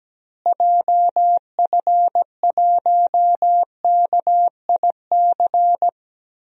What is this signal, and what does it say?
Morse code, heard as a keyed tone, about 17 wpm, 700 Hz. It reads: JF1KIC
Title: DXCC-100 Call Signs - 6 Characters - Repeated 3x - 17wpm